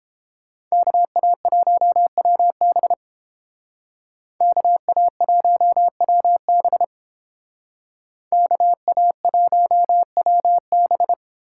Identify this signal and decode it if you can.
Morse code, heard as a keyed tone, about 31 wpm, 700 Hz. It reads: KA1WB KA1WB KA1WB